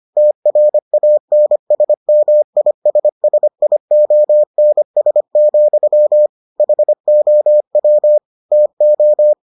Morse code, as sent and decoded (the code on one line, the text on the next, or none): - .-. .- -. ... -- .. ... ... .. --- -. ... --..-- / .... --- .-- / - ---
TRANSMISSIONS, HOW TO